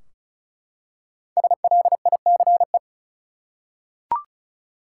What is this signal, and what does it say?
Morse code, heard as a keyed tone, about 36 wpm, 700 Hz. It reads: SLICE
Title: Five Letter Words - 35wpm